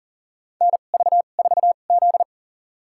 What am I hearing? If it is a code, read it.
Morse code, heard as a keyed tone, about 40 wpm, spaced as for 29 wpm, 700 Hz. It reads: NV4Z
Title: USA Call Signs - 4 Characters - 2x Character Spacing - 40wpm